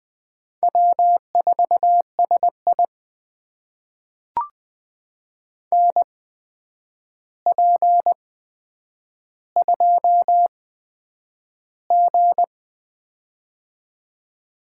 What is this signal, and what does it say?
Morse code, heard as a keyed tone, about 20 wpm, spaced as for 9 wpm, 700 Hz. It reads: W4SI NP2G
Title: USA Call Signs - 4 Characters - 8x Character Spacing - 20wpm